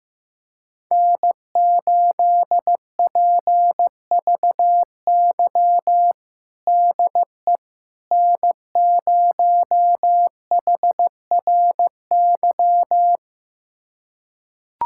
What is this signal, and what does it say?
Morse code, heard as a keyed tone, about 15 wpm, 700 Hz. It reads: N8PVY DE N0HRY
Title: USA Callsign DE Callsign - 5 Characters - 15wpm